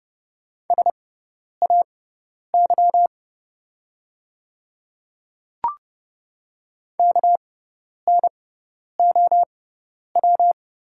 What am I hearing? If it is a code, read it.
Morse code, heard as a keyed tone, about 30 wpm, spaced as for 10 wpm, 700 Hz. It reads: SAY KNOW
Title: 6x Character Spacing - Top 200 Words - 30wpm